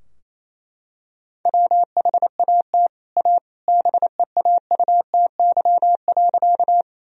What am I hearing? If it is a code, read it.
Morse code, heard as a keyed tone, about 28 wpm, 700 Hz. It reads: WHAT A BEAUTY.
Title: Sentences from Top 1000 Words - Review Part 2 - 28wpm